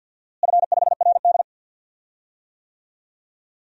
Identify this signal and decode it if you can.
Morse code, heard as a keyed tone, about 48 wpm, 700 Hz. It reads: F5RD
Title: DXCC-100 Call Signs - 4 Characters - Repeated 2x - 50wpm